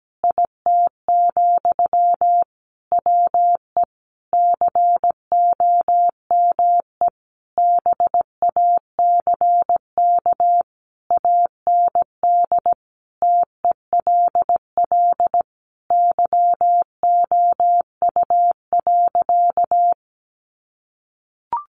Morse code, heard as a keyed tone, about 17 wpm, 700 Hz. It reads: IT, WE COME BACK AND TELL YOU.
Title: Sentences from Top 1100 Words 17wpm